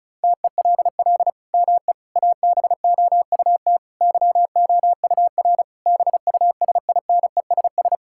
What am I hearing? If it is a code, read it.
Morse code, heard as a keyed tone, about 36 wpm, 700 Hz. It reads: TELL ME ABOUT YOUR BUSINESS
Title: Sentences from Top 1000 Words - Review Part 5 - 35wpm